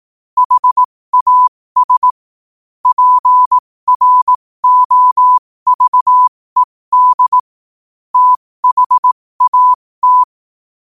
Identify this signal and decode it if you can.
Morse code, heard as a keyed tone, about 18 wpm, spaced as for 15 wpm, 1000 Hz. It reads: HAS PROVED THAT
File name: qotdfiles / 15wpm